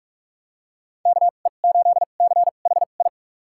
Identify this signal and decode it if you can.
Morse code, heard as a keyed tone, about 46 wpm, spaced as for 33 wpm, 700 Hz. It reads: KE8CHI
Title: USA Call Signs - 6 Characters - 2x Character Spacing - 45wpm